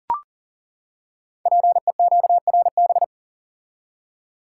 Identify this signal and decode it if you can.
Morse code, heard as a keyed tone, about 39 wpm, 700 Hz. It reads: PEQRB
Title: Code Groups - 5 Letters - 40wpm